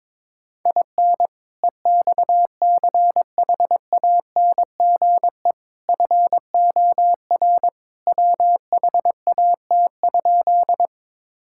Morse code, heard as a keyed tone, about 22 wpm, 700 Hz. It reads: IN EXCHANGE FOR WHAT?
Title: Sentences from Top 1000 Words - Review Part 6 - 22wpm